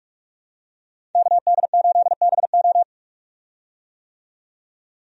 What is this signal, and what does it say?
Morse code, heard as a keyed tone, about 42 wpm, 700 Hz. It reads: KD8BO
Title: USA Call Signs - 5 Characters - 45wpm